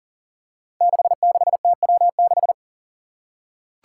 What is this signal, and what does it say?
Morse code, heard as a keyed tone, about 40 wpm, 700 Hz. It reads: 66TW6
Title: Five Challenging Letter-Number Combinations 40wpm